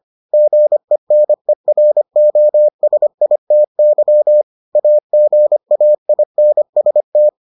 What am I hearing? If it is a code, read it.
Morse code, heard as a keyed tone, about 25 wpm, 600 Hz. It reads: GENEROSITY AGAINST